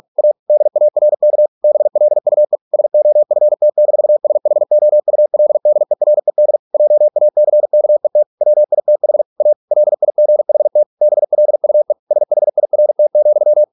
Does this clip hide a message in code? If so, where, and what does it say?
Yes: Morse code from 0.2 to 13.6 s, heard as a keyed tone, about 46 wpm, 600 Hz. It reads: A DARK BLUE SOFT-SHOULDERED JACKET WITH A LIGHT BLUE SHIRT,